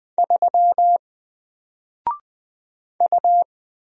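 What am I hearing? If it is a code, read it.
Morse code, heard as a keyed tone, about 20 wpm, 700 Hz. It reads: 3 U
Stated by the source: Single letter and number Morse drill at 20 wpm, from N4LCD.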